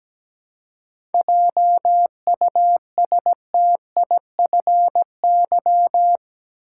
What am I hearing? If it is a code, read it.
Morse code, heard as a keyed tone, about 17 wpm, 700 Hz. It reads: JUSTIFY
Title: Seven Letter Words - 17wpm